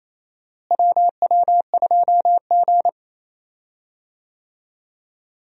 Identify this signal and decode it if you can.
Morse code, heard as a keyed tone, about 28 wpm, 700 Hz. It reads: WW2G
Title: USA Call Signs - 4 Characters - Repeated 2x - 28wpm